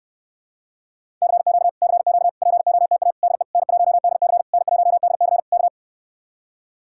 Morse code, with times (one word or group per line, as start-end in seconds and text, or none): 1.2-1.7 CQ
1.8-2.3 CQ
2.4-3.1 POTA
3.2-3.4 DE
3.5-4.4 N1DC
4.5-5.4 N1DC
5.5-5.7 K